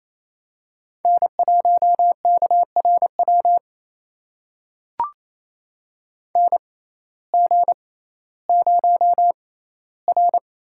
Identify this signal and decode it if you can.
Morse code, heard as a keyed tone, about 28 wpm, spaced as for 15 wpm, 700 Hz. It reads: N1KRW NG0R
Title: USA Call Signs - 6x Character Spacing - 28wpm